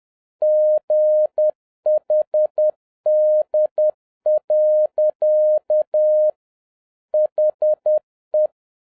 Morse code, heard as a keyed tone, about 10 wpm, 610 Hz. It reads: GHD. HE